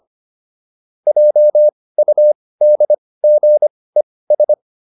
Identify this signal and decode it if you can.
Morse code, heard as a keyed tone, about 25 wpm, spaced as for 18 wpm, 600 Hz. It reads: JUDGES